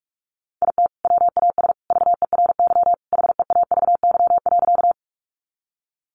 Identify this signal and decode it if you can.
Morse code, heard as a keyed tone, about 44 wpm, 700 Hz. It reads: IT WAS VERY HEAVY.
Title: Sentences from Top 1100 Words 45wpm